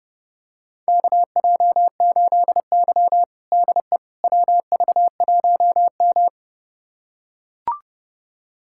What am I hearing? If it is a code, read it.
Morse code, heard as a keyed tone, about 30 wpm, 700 Hz. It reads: KJ8Y DE WV1M
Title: USA Callsign DE Callsign - 4 Characters - 30wpm